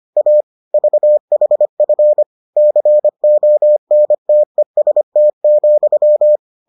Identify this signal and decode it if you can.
Morse code, heard as a keyed tone, about 25 wpm, 600 Hz. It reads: A VHF CONTEST,